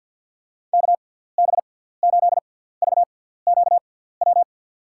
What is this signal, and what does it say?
Morse code, heard as a keyed tone, about 46 wpm, spaced as for 17 wpm, 700 Hz. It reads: KB8VQW
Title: USA Call Signs - 6 Characters - 6x Character Spacing - 50wpm